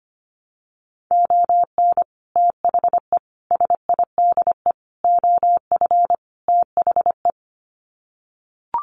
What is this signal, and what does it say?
Morse code, heard as a keyed tone, about 25 wpm, 700 Hz. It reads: ON THE SIDE OF THE
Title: Sets of 5 Words 25wpm